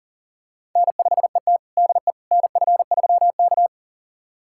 Morse code, heard as a keyed tone, about 39 wpm, 700 Hz. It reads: N5ET DE NF3K
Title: USA Callsign DE Callsign - 4 Characters - 40wpm